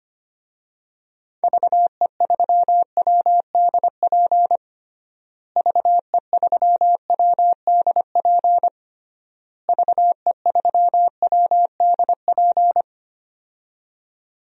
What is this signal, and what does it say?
Morse code, heard as a keyed tone, about 25 wpm, 700 Hz. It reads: VE3WDP VE3WDP VE3WDP
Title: DXCC-100 Call Signs - 6 Characters - Repeated 3x - 25wpm